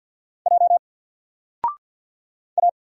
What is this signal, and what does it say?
Morse code, heard as a keyed tone, about 50 wpm, 700 Hz. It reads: J A
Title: Single Character (Letter:Number:Punctuation:Prosign) 50wpm